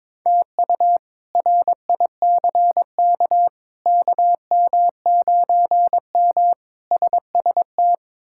Morse code, heard as a keyed tone, about 22 wpm, 700 Hz. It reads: TU RICK KM9M SST
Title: K1USN SST QSO Practice - 22wpm